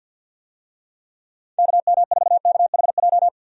Morse code, heard as a keyed tone, about 47 wpm, 700 Hz. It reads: KM4KHJ
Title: USA Call Signs - 6 Characters - 50wpm